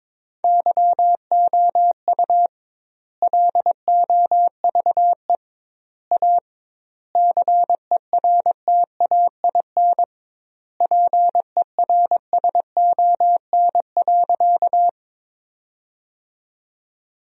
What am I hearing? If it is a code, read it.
Morse code, heard as a keyed tone, about 22 wpm, 700 Hz. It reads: YOU LOVE A CERTAIN PERSON.
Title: 2x Word-Spacing - Sentences from Top 500 Words - 22wpm